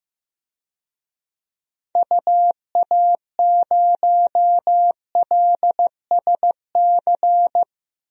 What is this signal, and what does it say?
Morse code, heard as a keyed tone, about 15 wpm, 700 Hz. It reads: UA0LSC
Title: DXCC-100 Call Signs - 6 Characters - Repeated 3x - 15wpm